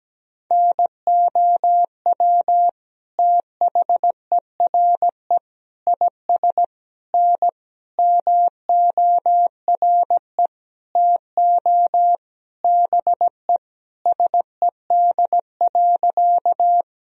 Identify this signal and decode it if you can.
Morse code, heard as a keyed tone, about 17 wpm, 700 Hz. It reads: NOW THERE IS N MORE TO BE SED.
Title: Sentences from Top 100 Words and Abbreviations 17wpm